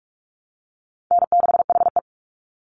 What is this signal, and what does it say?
Morse code, heard as a keyed tone, about 46 wpm, 700 Hz. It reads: N6HE
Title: Easy Call Signs Repeated 3x - 45wpm